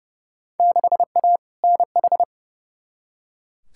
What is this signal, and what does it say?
Morse code, heard as a keyed tone, about 31 wpm, 700 Hz. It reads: 6A NH
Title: Field Day Exchanges - 30wpm